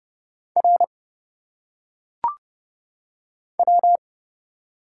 Morse code, heard as a keyed tone, about 30 wpm, 700 Hz. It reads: R W
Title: Single Letters - Phonetic Pronunciation 30wpm